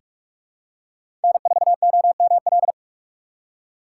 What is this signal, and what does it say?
Morse code, heard as a keyed tone, about 46 wpm, 700 Hz. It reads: N4OML
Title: USA Call Signs - 5 Characters - 3x Character Spacing - 45wpm